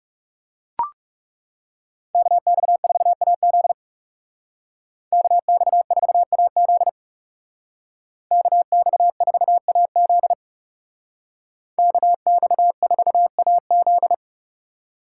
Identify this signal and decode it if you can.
Morse code, heard as a keyed tone, about 35 wpm, 700 Hz. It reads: KX4AZ KX4AZ KX4AZ KX4AZ
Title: Speed Racing - USA 2x2 Call Signs 30wpm